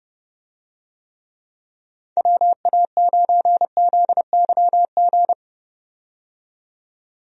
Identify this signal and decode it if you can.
Morse code, heard as a keyed tone, about 30 wpm, 700 Hz. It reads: WA9ZYG
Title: USA Call Signs - 6 Characters - 30wpm